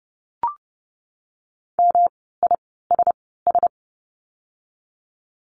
Morse code, heard as a keyed tone, about 30 wpm, spaced as for 17 wpm, 700 Hz. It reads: MISS
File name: course / 30:10wpm / 108 Lesson 18f - Letter m - Words 30wpm